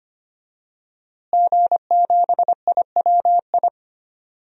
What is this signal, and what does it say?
Morse code, heard as a keyed tone, about 25 wpm, 700 Hz. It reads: G7IWI
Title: DXCC-100 Call Signs - 5 Characters - Repeated 3x - 25wpm